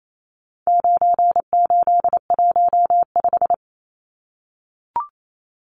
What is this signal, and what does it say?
Morse code, heard as a keyed tone, about 28 wpm, 700 Hz. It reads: 9815
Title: Numbers - 4-Digit 28wpm